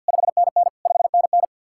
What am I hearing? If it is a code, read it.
Morse code, heard as a keyed tone, about 66 wpm, spaced as for 58 wpm, 700 Hz. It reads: 5NN 5NN
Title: Speed-Racing Common QSO Phrases - 50wpm